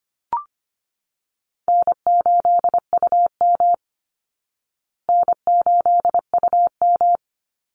Morse code, heard as a keyed tone, about 25 wpm, 700 Hz. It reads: N8UM N8UM
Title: Call Signs Repeated 3x - 25wpm